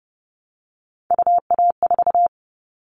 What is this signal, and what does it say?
Morse code, heard as a keyed tone, about 30 wpm, 700 Hz. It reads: UA4
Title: Callsign Prefixes for Top 100 DXCC - 30wpm